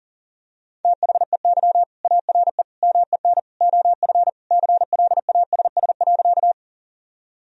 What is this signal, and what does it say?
Morse code, heard as a keyed tone, about 40 wpm, 700 Hz. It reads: THEY ARE MEN OF CLASS.
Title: Sentences from Top 1000 Words - Review Part 2 - 40wpm